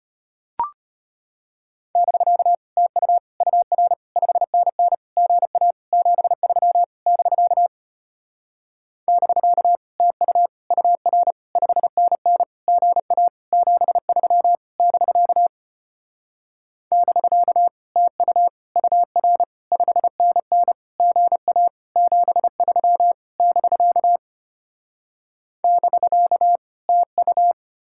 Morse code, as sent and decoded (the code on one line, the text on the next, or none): -...-.- / - ..- / ..- .-. / ..... -. -. / --. .- / --... ...-- / -...-.- / -...-.- / - ..- / ..- .-. / ..... -. -. / --. .- / --... ...-- / -...-.- / -...-.- / - ..- / ..- .-. / ..... -. -. / --. .- / --... ...-- / -...-.- / -...-.- / - ..-
<BK> TU UR 5NN GA 73 <BK> <BK> TU UR 5NN GA 73 <BK> <BK> TU UR 5NN GA 73 <BK> <BK> TU